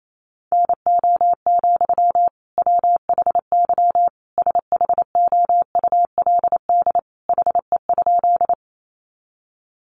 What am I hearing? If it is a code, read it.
Morse code, heard as a keyed tone, about 28 wpm, 700 Hz. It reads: NO, WHY SHOULD HE?
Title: Sentences from Top 300 Words 28wpm